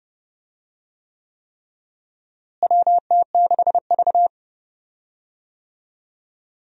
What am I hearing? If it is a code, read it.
Morse code, heard as a keyed tone, about 29 wpm, 700 Hz. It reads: WT6V
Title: USA Call Signs - 4 Characters - Repeated 3x - 30wpm